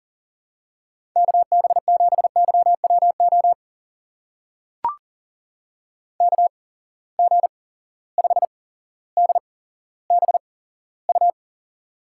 Morse code, heard as a keyed tone, about 40 wpm, spaced as for 17 wpm, 700 Hz. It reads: KB7YWO KG5DBU
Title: USA Call Signs - 6 Characters - 8x Character Spacing - 40wpm